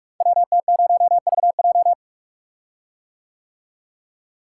Morse code, heard as a keyed tone, about 44 wpm, 700 Hz. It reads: WT0VJ
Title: Code Groups - 5 Letter:Numbers - 45wpm